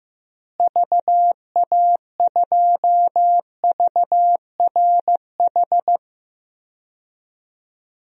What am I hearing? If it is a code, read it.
Morse code, heard as a keyed tone, about 15 wpm, 700 Hz. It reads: VA2VRH
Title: DXCC-100 Call Signs - 6 Characters - Repeated 3x - 15wpm